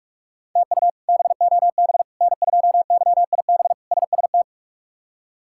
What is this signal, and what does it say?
Morse code, heard as a keyed tone, about 41 wpm, 700 Hz. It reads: TU BOB N2YIB SST
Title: K1USN SST QSO Practice - 45wpm